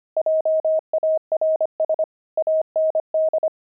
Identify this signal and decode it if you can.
Morse code, heard as a keyed tone, about 25 wpm, 630 Hz. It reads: JARS AND
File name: pangram-fs-25-cs-25-twelve